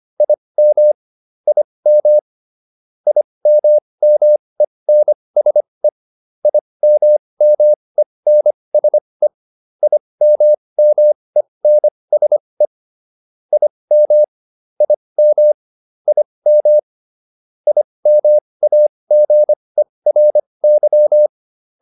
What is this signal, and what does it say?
Morse code, heard as a keyed tone, about 25 wpm, spaced as for 20 wpm, 600 Hz. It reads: IM IM IMMENSE IMMENSE IMMENSE IM IM IM IMAGERY